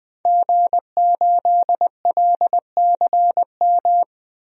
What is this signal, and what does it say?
Morse code, heard as a keyed tone, about 20 wpm, 700 Hz. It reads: G8LCM
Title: DXCC-100 Call Signs - 5 Characters - 20wpm